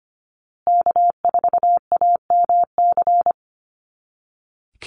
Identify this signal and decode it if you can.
Morse code, heard as a keyed tone, about 25 wpm, 700 Hz. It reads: K4AMC
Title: Call Signs Repeated 2x - 25wpm